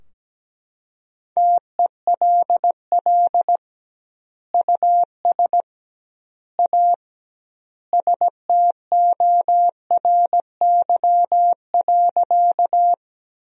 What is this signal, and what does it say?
Morse code, heard as a keyed tone, about 17 wpm, 700 Hz. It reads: TELL US A STORY.
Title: 2x Word-Spacing - Sentences from Top 300 Words - 17wpm